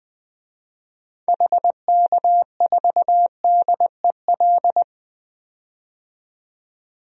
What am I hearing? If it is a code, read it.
Morse code, heard as a keyed tone, about 20 wpm, 700 Hz. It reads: HK4DEL
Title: DXCC-100 Call Signs - 6 Characters - 20wpm